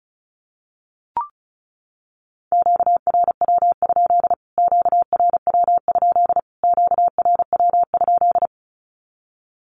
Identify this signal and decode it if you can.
Morse code, heard as a keyed tone, about 34 wpm, 700 Hz. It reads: QRW? QRW? QRW?
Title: Q-Codes Repeated 3x 35wpm